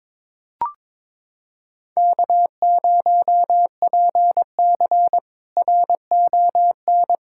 Morse code, heard as a keyed tone, about 22 wpm, 700 Hz. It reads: K0PC RON
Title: K1USN SST QSO Practice - 22wpm